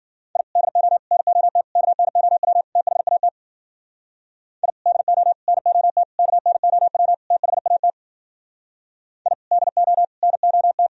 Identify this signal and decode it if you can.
Morse code, heard as a keyed tone, about 51 wpm, 700 Hz. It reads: I DO NOT KNOW THAT I DO NOT KNOW THAT I DO NOT